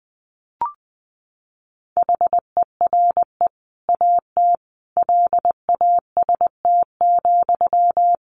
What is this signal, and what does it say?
Morse code, heard as a keyed tone, about 20 wpm, 700 Hz. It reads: HERE AT LAST,
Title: Sentences from Top 1100 Words 20wpm